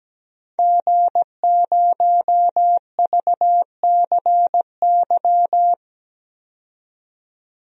Morse code, heard as a keyed tone, about 17 wpm, 700 Hz. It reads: G0VCY
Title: DXCC-100 Call Signs - 5 Characters - 17wpm